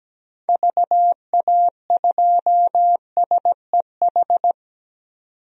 Morse code, heard as a keyed tone, about 17 wpm, 700 Hz. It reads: VA2SEH
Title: DXCC-100 Call Signs - 6 Characters - 17wpm